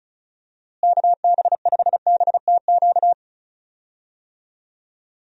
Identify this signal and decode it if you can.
Morse code, heard as a keyed tone, about 34 wpm, 700 Hz. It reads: KB5BTQ
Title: USA Call Signs - 6 Characters - Repeated 3x - 35wpm